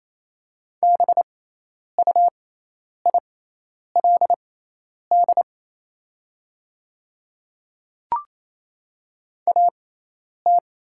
Morse code, heard as a keyed tone, about 28 wpm, spaced as for 10 wpm, 700 Hz. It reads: BUILD AT
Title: 6x Character Spacing - Top 200 Words - 28wpm